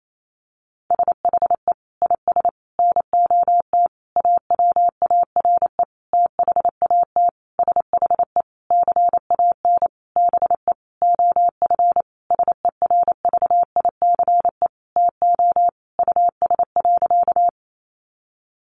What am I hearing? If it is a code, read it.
Morse code, heard as a keyed tone, about 28 wpm, 700 Hz. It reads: SHE IS NOT AWARE THAT SHE CAN BE OF SERVICE TO US.
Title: Sentences from Top 1200 Words 28wpm